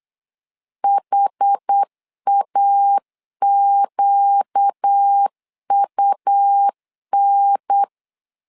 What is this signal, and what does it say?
Morse code, heard as a keyed tone, about 8 wpm, 800 Hz. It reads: HAQUN
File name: LEZIONI / 5 LBVC / 5 LB 8wpm 2816